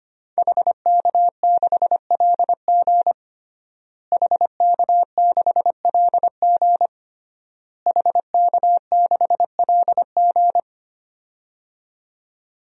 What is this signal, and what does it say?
Morse code, heard as a keyed tone, about 25 wpm, 700 Hz. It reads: HK6LG HK6LG HK6LG
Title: DXCC-100 Call Signs - 5 Characters - Repeated 3x - 25wpm